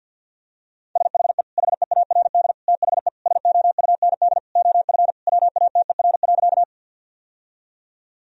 Morse code, heard as a keyed tone, about 53 wpm, 700 Hz. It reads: SHE HEARD THE SOUND OF WATER.
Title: Sentences from Top 1000 Words - Review Part 1 - 50wpm